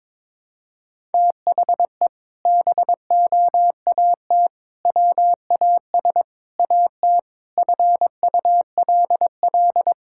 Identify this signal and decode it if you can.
Morse code, heard as a keyed tone, about 22 wpm, 700 Hz. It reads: THE BOAT WAS AT FULL
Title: Sentences from Top 1000 Words - Review Part 5 - 22wpm